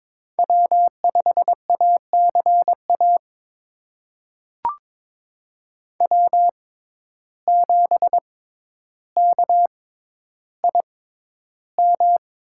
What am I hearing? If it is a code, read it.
Morse code, heard as a keyed tone, about 22 wpm, spaced as for 11 wpm, 700 Hz. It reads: W5ACA W7KIM
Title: USA Call Signs - 5 Characters - 6x Character Spacing - 22wpm